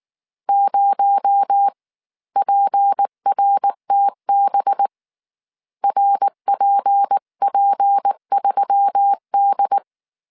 Morse code, heard as a keyed tone, about 19 wpm, spaced as for 16 wpm, 800 Hz. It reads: 0PRTB RPP3D
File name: CW / DET / 1 6 1500C S C 18wpm 2356